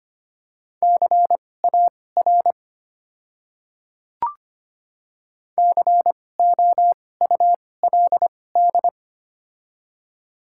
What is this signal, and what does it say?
Morse code, heard as a keyed tone, about 25 wpm, spaced as for 18 wpm, 690 Hz. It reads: CAR COULD
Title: 2x Character Spacing - Top 300 Words - 25wpm